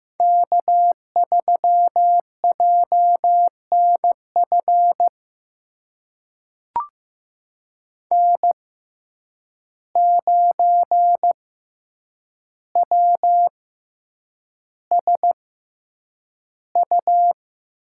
Morse code, heard as a keyed tone, about 15 wpm, spaced as for 8 wpm, 700 Hz. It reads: K3JNF N9WSU
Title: USA Call Signs - 5 Characters - 6x Character Spacing - 15wpm